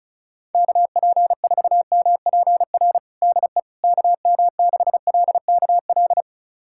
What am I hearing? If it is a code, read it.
Morse code, heard as a keyed tone, about 36 wpm, 700 Hz. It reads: KP4MPR DE KM6LKL
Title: USA Callsign DE Callsign - 6 Characters - 35wpm